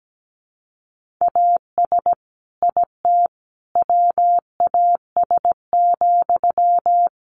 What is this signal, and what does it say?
Morse code, heard as a keyed tone, about 17 wpm, 700 Hz. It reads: AS IT WAS,
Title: Sentences from Top 1400 Words 17wpm